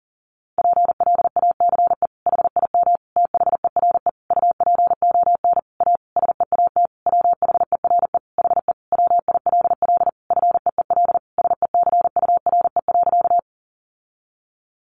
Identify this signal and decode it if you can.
Morse code, heard as a keyed tone, about 40 wpm, 700 Hz. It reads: PLACE HIM THERE UPON A SEAT WHERE HE WILL FEEL SECURE.
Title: Sentences from Top 1500 Words 40wpm